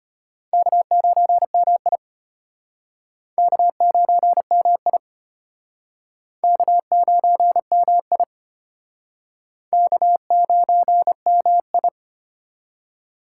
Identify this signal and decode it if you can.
Morse code, heard as a keyed tone, about 34 wpm, 700 Hz. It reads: K9MI K9MI K9MI K9MI